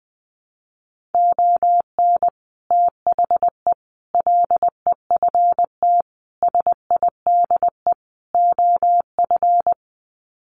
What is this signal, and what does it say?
Morse code, heard as a keyed tone, about 20 wpm, 700 Hz. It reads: ON THE LEFT SIDE OF
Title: Sets of 5 Words Encore2 20wpm